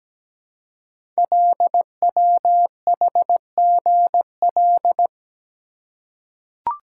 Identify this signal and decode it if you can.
Morse code, heard as a keyed tone, about 17 wpm, 700 Hz. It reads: LWHGL
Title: Code Groups - 5 Letter:Numbers - 17wpm